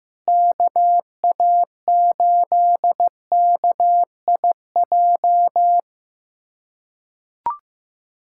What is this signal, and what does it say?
Morse code, heard as a keyed tone, about 15 wpm, 700 Hz. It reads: KA8KIJ
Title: USA Callsign DE Callsign - 6 Characters - 15wpm